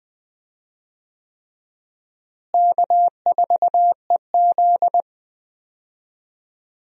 Morse code, heard as a keyed tone, about 20 wpm, 700 Hz. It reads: K4EZ
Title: USA Call Signs - 4 Characters - Repeated 3x - 20wpm